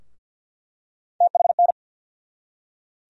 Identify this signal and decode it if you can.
Morse code, heard as a keyed tone, about 47 wpm, 700 Hz. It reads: THN
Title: USA Call Sign Postfixes - 3 Letters 50wpm